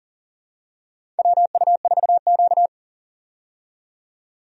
Morse code, heard as a keyed tone, about 40 wpm, 700 Hz. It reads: WU4Q
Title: USA Call Signs - 4 Characters - 40wpm